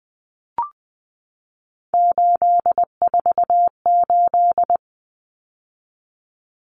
Morse code, heard as a keyed tone, about 20 wpm, 700 Hz. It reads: 848
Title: Common Cut Numbers - 3-Digit 20wpm